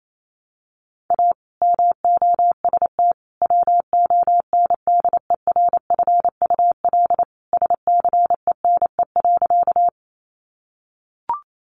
Morse code, heard as a keyed tone, about 28 wpm, 700 Hz. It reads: A MOST WONDERFUL SCENE.
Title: Sentences from Top 1100 Words 28wpm